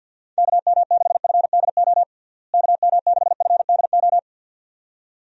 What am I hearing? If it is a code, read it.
Morse code, heard as a keyed tone, about 51 wpm, 700 Hz. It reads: KM6FDO KM6FDO
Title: USA Call Signs - 6 Characters - Repeated 2x - 50wpm